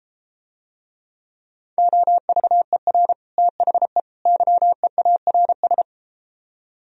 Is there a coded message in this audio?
Yes: Morse code from 1.8 to 5.8 s, heard as a keyed tone, about 33 wpm, 700 Hz. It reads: OVER THE YEARS